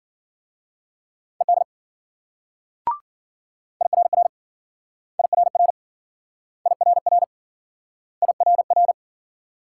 Morse code, heard as a keyed tone, about 52 wpm, 700 Hz. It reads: ED IRR IRR IRR IRR